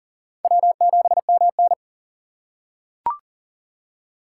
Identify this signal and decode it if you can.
Morse code, heard as a keyed tone, about 40 wpm, 700 Hz. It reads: W7MN
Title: USA Call Signs - 4x Character Spacing - 40wpm